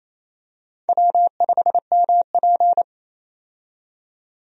Morse code, heard as a keyed tone, about 28 wpm, 700 Hz. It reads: W5MP
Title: USA Call Signs - 4 Characters - 28wpm